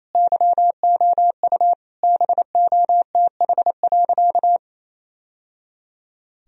Morse code, heard as a keyed tone, about 27 wpm, 700 Hz. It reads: YOU BOTH.
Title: Sentences from Top 1000 Words - Review Part 1 - 28wpm